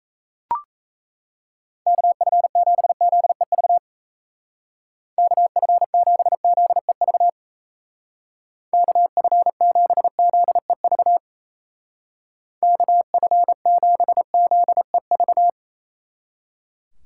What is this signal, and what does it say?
Morse code, heard as a keyed tone, about 35 wpm, 700 Hz. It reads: KF7ZEV KF7ZEV KF7ZEV KF7ZEV
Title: Speed Racing - USA 2x3 Call Signs 28wpm